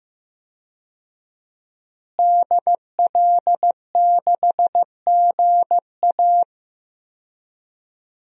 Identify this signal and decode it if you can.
Morse code, heard as a keyed tone, about 15 wpm, 700 Hz. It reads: DL6GA
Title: DXCC-100 Call Signs - 5 Characters - Repeated 2x - 15wpm